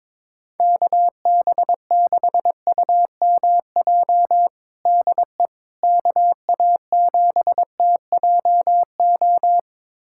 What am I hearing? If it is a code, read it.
Morse code, heard as a keyed tone, about 22 wpm, 700 Hz. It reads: KB6UMJ DE KA7TJO
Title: USA Callsign DE Callsign - 6 Characters - 22wpm